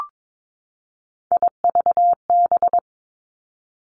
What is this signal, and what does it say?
Morse code, heard as a keyed tone, about 22 wpm, 700 Hz. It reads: IVB
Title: Three Challenging Letter-Number Combinations 22wpm